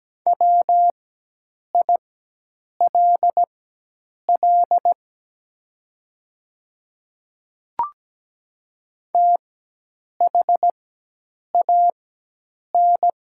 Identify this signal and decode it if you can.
Morse code, heard as a keyed tone, about 17 wpm, spaced as for 8 wpm, 700 Hz. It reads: WILL THAN
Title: 4x Character Spacing - Top 100 Words - 17wpm